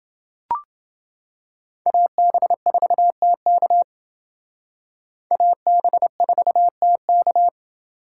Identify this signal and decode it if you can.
Morse code, heard as a keyed tone, about 28 wpm, 700 Hz. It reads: AB4TK AB4TK